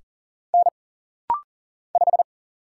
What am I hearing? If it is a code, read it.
Morse code, heard as a keyed tone, about 40 wpm, 700 Hz. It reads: N 5
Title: Single Letter-Number - Rapid-Fire 40wpm